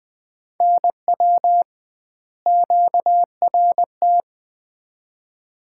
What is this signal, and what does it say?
Morse code, heard as a keyed tone, about 20 wpm, 700 Hz. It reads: NW QRT
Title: Common QSO Phrases - 2x Word-Spacing - 20wpm